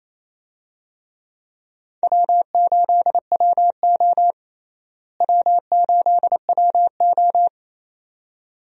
Morse code, heard as a keyed tone, about 28 wpm, 700 Hz. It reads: W8WO W8WO
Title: USA Call Signs - 4 Characters - Repeated 2x - 28wpm